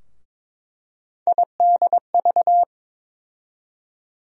Morse code, heard as a keyed tone, about 23 wpm, 700 Hz. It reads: IDV
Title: USA Call Sign Postfixes - 3 Letters 22wpm